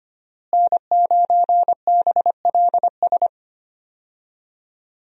Morse code, heard as a keyed tone, about 25 wpm, 700 Hz. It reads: N9BLS
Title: USA Call Signs - 5 Characters - Repeated 2x - 25wpm